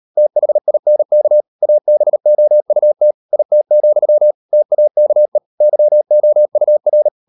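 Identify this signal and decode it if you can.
Morse code, heard as a keyed tone, about 38 wpm, 600 Hz. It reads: THINK ABOUT IT, TAKE YOUR
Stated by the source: Morse code practice at 38 WPM, sending Ken Follett's novel Never in English, from CW med Gnister.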